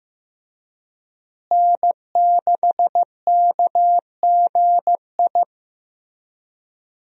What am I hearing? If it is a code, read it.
Morse code, heard as a keyed tone, about 15 wpm, 700 Hz. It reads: N6KGI
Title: USA Call Signs - 5 Characters - Repeated 2x - 15wpm